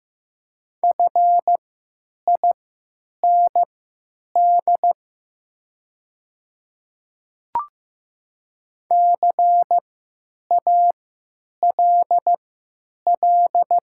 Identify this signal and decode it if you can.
Morse code, heard as a keyed tone, about 15 wpm, spaced as for 8 wpm, 700 Hz. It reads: FIND CALL
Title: 3x Character Spacing - Top 100 Words - 15wpm